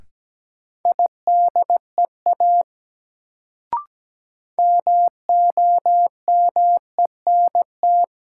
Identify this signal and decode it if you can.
Morse code, heard as a keyed tone, about 17 wpm, 700 Hz. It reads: IDEA MOMENT
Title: Sets of 3 Words - New Word Drill - 17wpm